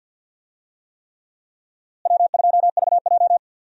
Morse code, heard as a keyed tone, about 53 wpm, 700 Hz. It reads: W2VJ
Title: USA Call Signs - 4 Characters - Repeated 3x - 50wpm